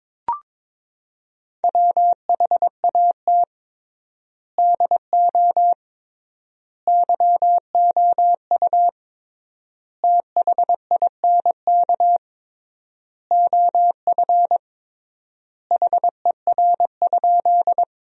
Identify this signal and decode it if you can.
Morse code, heard as a keyed tone, about 22 wpm, 700 Hz. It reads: WHAT DO YOU THINK OF HER?
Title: 3x Word-Spacing - Sentences from Top 200 Words - 22wpm